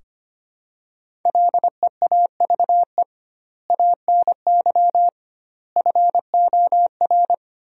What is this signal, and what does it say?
Morse code, heard as a keyed tone, about 25 wpm, 700 Hz. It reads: LEAVE ANY FOR
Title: 2x Word-Spacing - Sentences from Top 400 Words - 25wpm